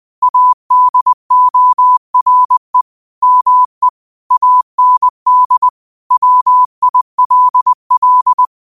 Morse code, heard as a keyed tone, about 20 wpm, 1000 Hz. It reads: ADORE ME AND WILL
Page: Quotes for Fri, 15 Aug 2025 in Morse Code at 20 words per minute.